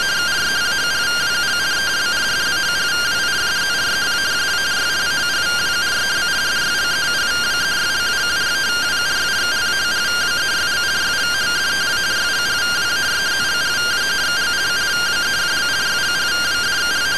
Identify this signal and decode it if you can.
Two-tone teleprinter signal, about 170 Hz shift.